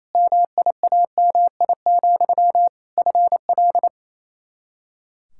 Morse code, heard as a keyed tone, about 27 wpm, 700 Hz. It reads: MIAMI, FL
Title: K1USN SST QSO Practice 28wpm